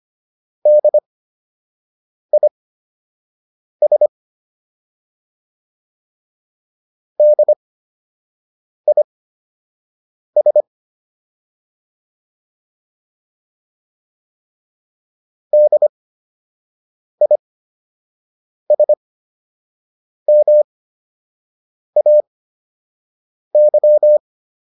Morse code, heard as a keyed tone, about 25 wpm, spaced as for 6 wpm, 600 Hz. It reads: DIS DIS DISMAY